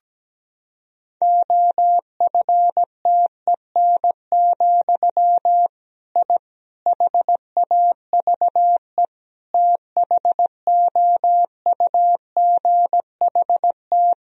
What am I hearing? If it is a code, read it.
Morse code, heard as a keyed tone, about 17 wpm, 700 Hz. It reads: OFTEN, I HAVE THOUGHT
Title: Sentences from Top 1000 Words - Review Part 1 - 17wpm